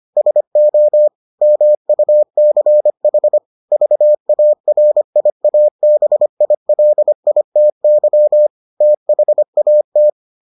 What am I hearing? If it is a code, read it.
Morse code, heard as a keyed tone, about 25 wpm, 600 Hz. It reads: SO MUCH VARIABILITY THAT